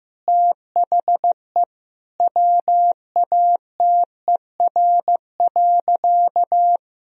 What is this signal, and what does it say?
Morse code, heard as a keyed tone, about 15 wpm, 700 Hz. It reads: THE WATER.
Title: Sentences from Top 500 Words and Abbreviations 15wpm